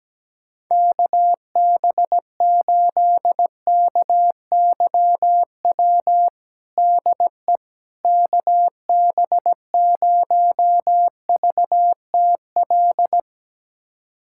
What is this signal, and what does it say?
Morse code, heard as a keyed tone, about 17 wpm, 700 Hz. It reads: KB8KYW DE KB0VTL
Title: USA Callsign DE Callsign 17wpm